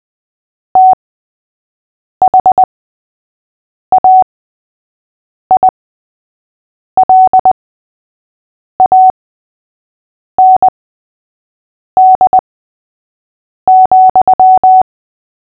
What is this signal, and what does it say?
Morse code, heard as a keyed tone, about 20 wpm, spaced as for 6 wpm, 750 Hz. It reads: THAILAND,
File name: cw / fists / Short-Story-3-6wpm